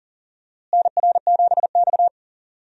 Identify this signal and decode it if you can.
Morse code, heard as a keyed tone, about 40 wpm, 700 Hz. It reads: NR7X
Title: USA Call Signs - 4 Characters - Repeated 3x - 40wpm